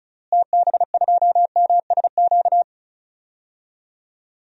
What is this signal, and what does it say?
Morse code, heard as a keyed tone, about 36 wpm, 700 Hz. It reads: TB2MSQ
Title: DXCC-100 Call Signs - 6 Characters - Repeated 3x - 35wpm